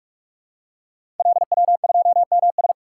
Morse code, heard as a keyed tone, about 44 wpm, 700 Hz. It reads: LW2MS
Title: Code Groups - 5 Letter:Numbers - 45wpm